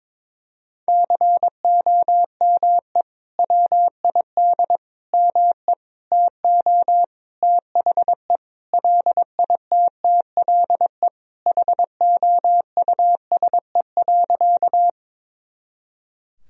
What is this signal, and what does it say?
Morse code, heard as a keyed tone, about 22 wpm, 700 Hz. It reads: COME WID ME TO THE LITTLE HOUSE.
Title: Sentences from Top 200 Words and Abbreviations 22wpm